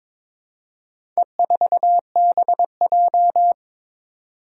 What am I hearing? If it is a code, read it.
Morse code, heard as a keyed tone, about 22 wpm, 700 Hz. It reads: E4BJ
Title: Four Challenging Letter-Number Combinations 22wpm